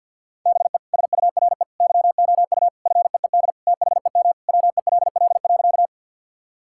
Morse code, heard as a keyed tone, about 50 wpm, 700 Hz. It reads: BE SURE YOU FEED THEM WELL.
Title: Sentences from Top 1600 Words 50wpm